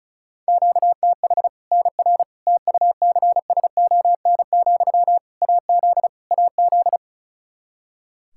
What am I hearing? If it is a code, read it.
Morse code, heard as a keyed tone, about 36 wpm, 700 Hz. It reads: QTH NR TUCSON, AZ AZ
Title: Practice QSO - Exchange 1 - Beginning - 35wpm